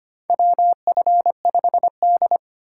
Morse code, heard as a keyed tone, about 25 wpm, 700 Hz. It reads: WF5D
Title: USA Call Signs - 4 Characters - 4x Character Spacing - 25wpm